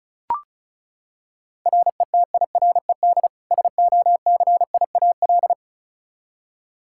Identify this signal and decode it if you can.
Morse code, heard as a keyed tone, about 34 wpm, 700 Hz. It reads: RETIRED SOCIAL
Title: K1USN SST QSO Practice 35wpm